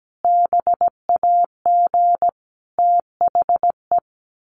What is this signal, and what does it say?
Morse code, heard as a keyed tone, about 17 wpm, 700 Hz. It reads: BAG THE
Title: A Sound Of Thunder 17wpm